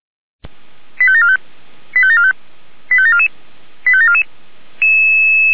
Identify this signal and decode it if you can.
Five-tone selective call (ZVEI), heard as quick runs of five tones, tones of about 70 ms. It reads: 85745 85745 85744 85744